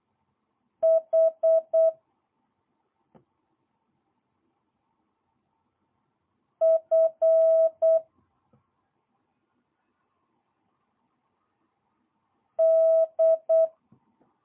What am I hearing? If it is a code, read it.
Morse code, heard as a keyed tone, about 8 wpm, 650 Hz. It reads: H F D
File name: jh8cbh / 2022cw